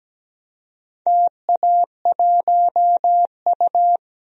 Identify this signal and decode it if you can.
Morse code, heard as a keyed tone, about 17 wpm, 700 Hz. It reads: TA1U
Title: DXCC-100 Call Signs - 4 Characters - 17wpm